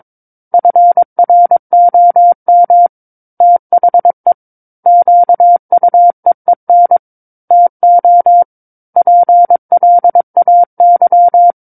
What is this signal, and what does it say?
Morse code, heard as a keyed tone, about 22 wpm, 700 Hz. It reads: FROM THE QUEEN TO PLAY
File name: 06-Chapter-06 22 WPM